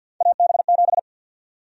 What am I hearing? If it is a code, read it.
Morse code, heard as a keyed tone, about 53 wpm, 700 Hz. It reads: AB7
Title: USA Call Sign Prefixes - 50wpm